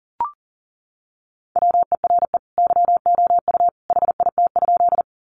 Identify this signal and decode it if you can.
Morse code, heard as a keyed tone, about 40 wpm, 700 Hz. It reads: WERE YOU HIT?
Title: Sentences from Top 1200 Words 40wpm